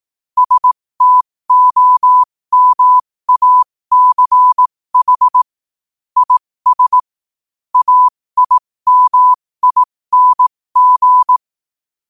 Morse code, heard as a keyed tone, about 18 wpm, spaced as for 15 wpm, 1000 Hz. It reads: STOMACH IS AIMING
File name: qotdfiles / 15wpm